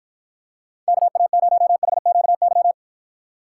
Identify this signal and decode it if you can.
Morse code, heard as a keyed tone, about 52 wpm, 700 Hz. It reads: KA0HQY